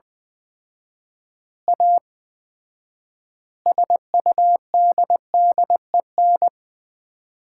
Morse code, heard as a keyed tone, about 21 wpm, 700 Hz. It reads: A SUDDEN
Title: 4x Word-Spacing - Sets of 4 Words - 20wpm